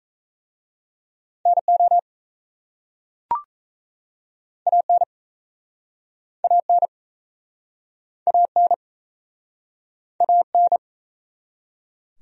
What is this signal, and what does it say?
Morse code, heard as a keyed tone, about 38 wpm, 700 Hz. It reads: NO AN AN AN AN